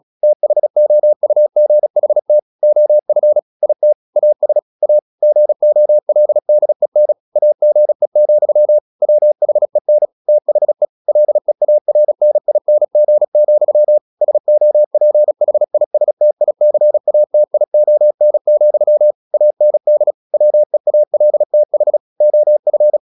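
Morse code, heard as a keyed tone, about 36 wpm, 600 Hz. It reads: THOUGHT OF IT AS A GOLDEN AGE, WHEN THE LEARNING, SOPHISTICATION, AND WEALTH OF
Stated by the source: Morse practice at 36 wpm, text of Ken Follett's novel Never, from CW med Gnister.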